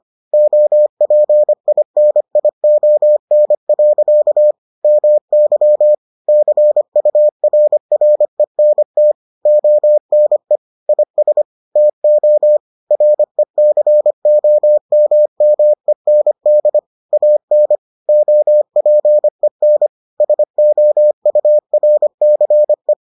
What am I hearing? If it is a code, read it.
Morse code, heard as a keyed tone, about 25 wpm, 600 Hz. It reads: OPINION. MY CURRENT ONE IS TO RECOMMEND AN OPEN SOURCE